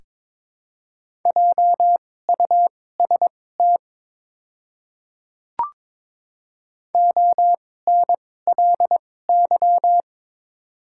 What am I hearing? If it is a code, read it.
Morse code, heard as a keyed tone, about 23 wpm, spaced as for 16 wpm, 700 Hz. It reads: JUST ONLY
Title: 2x Character Spacing - Top 300 Words - 22wpm